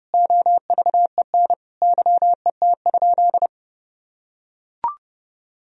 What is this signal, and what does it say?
Morse code, heard as a keyed tone, about 30 wpm, 700 Hz. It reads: OVEN YET?
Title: Sentences from Top 1000 Words - Review Part 4 - 30wpm